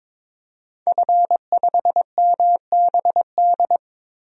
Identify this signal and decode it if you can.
Morse code, heard as a keyed tone, about 22 wpm, 700 Hz. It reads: F5MBD
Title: DXCC-100 Call Signs - 5 Characters - 22wpm